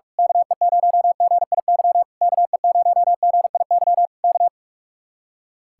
Morse code, heard as a keyed tone, about 46 wpm, 700 Hz. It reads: KE0GIY KE0GIY K
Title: Speed-Racing Common QSO Phrases - 45wpm